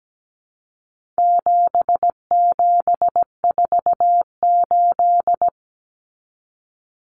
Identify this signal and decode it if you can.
Morse code, heard as a keyed tone, about 17 wpm, 700 Hz. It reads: 7748
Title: Common Cut Numbers - 4-Digit 17wpm